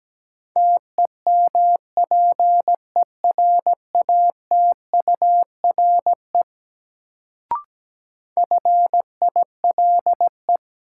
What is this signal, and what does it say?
Morse code, heard as a keyed tone, about 17 wpm, 700 Hz. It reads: TEMPERATURE FILE
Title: Sentences from Top 1000 Words - Review Part 4 - New Word Drill - 17wpm